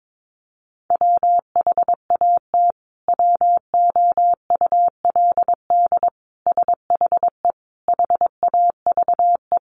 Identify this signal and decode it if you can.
Morse code, heard as a keyed tone, about 22 wpm, 700 Hz. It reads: WHAT WOULD SHE HAVE
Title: Sentences from Top 100 Words 22wpm